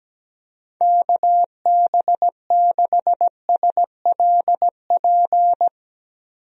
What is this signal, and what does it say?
Morse code, heard as a keyed tone, about 17 wpm, 700 Hz. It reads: KB6SLP
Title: USA Call Signs - 6 Characters - 17wpm